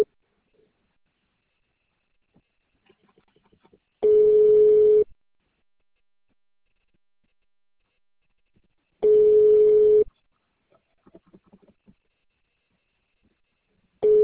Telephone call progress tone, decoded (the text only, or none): ringback tone